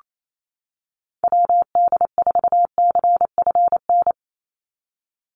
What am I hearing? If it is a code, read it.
Morse code, heard as a keyed tone, about 28 wpm, 700 Hz. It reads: WD4CFN
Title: NA SOTA Chaser Callsigns 28wpm